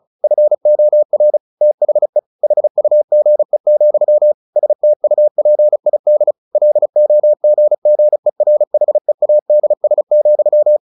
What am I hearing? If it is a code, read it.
Morse code, heard as a keyed tone, about 34 wpm, 600 Hz. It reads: FOR THE HUGE, STUPID LOGGERHEADS,
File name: the old man and the sea / the old man and the sea 1 35wpm